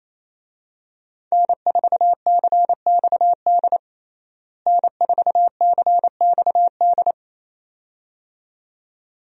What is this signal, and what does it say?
Morse code, heard as a keyed tone, about 28 wpm, 700 Hz. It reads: N4CXD N4CXD
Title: USA Call Signs - 5 Characters - Repeated 2x - 28wpm